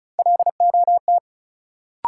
Morse code, heard as a keyed tone, about 36 wpm, 700 Hz. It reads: LOT
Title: Sentences from Top 500 Words - New Word Drill - 35wpm